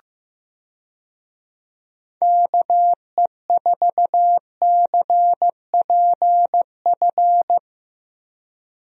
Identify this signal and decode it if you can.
Morse code, heard as a keyed tone, about 16 wpm, 700 Hz. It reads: KE4CPF
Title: USA Call Signs - 6 Characters - Repeated 2x - 15wpm